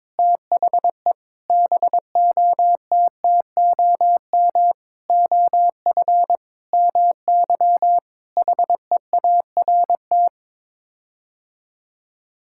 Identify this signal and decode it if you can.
Morse code, heard as a keyed tone, about 22 wpm, 700 Hz. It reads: THE BOTTOM OF MY HEART
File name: practice / 22wpm / Sets of 8 Words - 22wpm